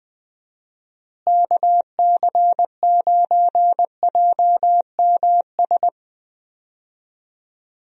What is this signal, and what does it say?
Morse code, heard as a keyed tone, about 21 wpm, 700 Hz. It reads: KC9JMS
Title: USA Call Signs - 6 Characters - Repeated 2x - 20wpm